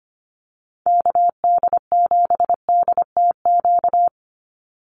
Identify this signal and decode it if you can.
Morse code, heard as a keyed tone, about 25 wpm, 700 Hz. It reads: KD7DTQ
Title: USA Call Signs 25wpm